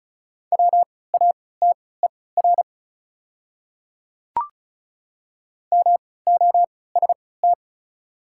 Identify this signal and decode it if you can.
Morse code, heard as a keyed tone, about 36 wpm, spaced as for 20 wpm, 700 Hz. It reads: WATER MOST
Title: 3x Character Spacing - Top 100 Words - 35wpm